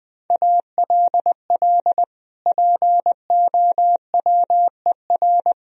Morse code, heard as a keyed tone, about 21 wpm, 700 Hz. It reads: ALL POWER
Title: Sentences from Top 1000 Words - Review Part 4 - 20wpm